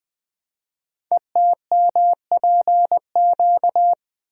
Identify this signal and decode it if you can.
Morse code, heard as a keyed tone, about 20 wpm, 700 Hz. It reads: ETMPQ
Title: Code Groups - 5 Letters - 20wpm